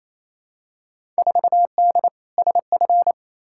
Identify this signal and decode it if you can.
Morse code, heard as a keyed tone, about 28 wpm, 700 Hz. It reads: 4D SF
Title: Field Day Exchanges Encore - 28wpm